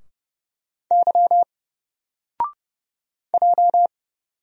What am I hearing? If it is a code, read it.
Morse code, heard as a keyed tone, about 30 wpm, 700 Hz. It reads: Y J
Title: Single Letter-Number Alt 30wpm